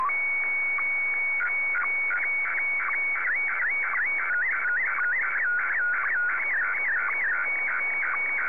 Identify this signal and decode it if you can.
SSTV (slow-scan television), Scottie DX.